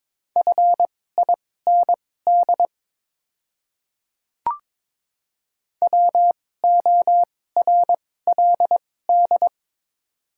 Morse code, heard as a keyed tone, about 22 wpm, spaced as for 16 wpm, 700 Hz. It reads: FIND WORLD
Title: 2x Character Spacing - Top 300 Words - 22wpm